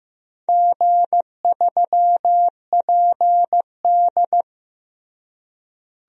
Morse code, heard as a keyed tone, about 15 wpm, 700 Hz. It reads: G3PD
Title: DXCC-100 Call Signs - 4 Characters - Repeated 2x - 15wpm